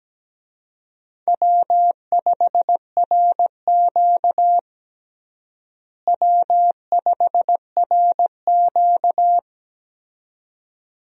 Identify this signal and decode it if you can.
Morse code, heard as a keyed tone, about 17 wpm, 700 Hz. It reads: W5RQ W5RQ
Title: USA Call Signs - 4 Characters - Repeated 2x - 17wpm